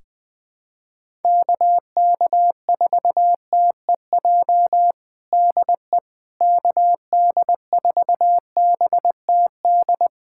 Morse code, heard as a keyed tone, about 20 wpm, 700 Hz. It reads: KK4TEJ DE KD4BTD
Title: USA Callsign DE Callsign - 6 Characters - 20wpm